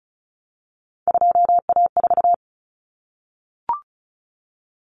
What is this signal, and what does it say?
Morse code, heard as a keyed tone, about 34 wpm, 700 Hz. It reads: 2A4
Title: Common Cut Numbers - 3-Digit 35wpm